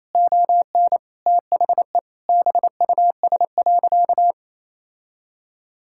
Morse code, heard as a keyed tone, about 28 wpm, 700 Hz. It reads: ON THE BUS.
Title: Sentences from Top 1000 Words - Review Part 6 - 28wpm